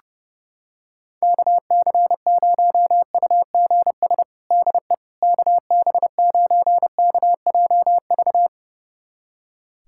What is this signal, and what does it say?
Morse code, heard as a keyed tone, about 30 wpm, 700 Hz. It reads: KC0UGS DE KB9KJV
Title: USA Callsign DE Callsign - 6 Characters - 30wpm